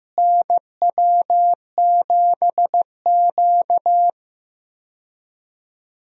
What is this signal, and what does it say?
Morse code, heard as a keyed tone, about 15 wpm, 700 Hz. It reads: NW7Q
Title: USA Call Signs - 4 Characters - Repeated 3x - 15wpm